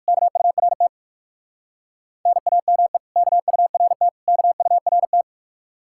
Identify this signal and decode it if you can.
Morse code, heard as a keyed tone, about 58 wpm, 700 Hz. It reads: KURT NAME KURT KURT